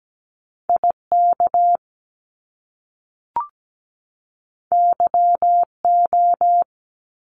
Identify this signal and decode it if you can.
Morse code, heard as a keyed tone, about 17 wpm, 700 Hz. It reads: IK YO
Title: Top Two Letter Combinations in Words 17wpm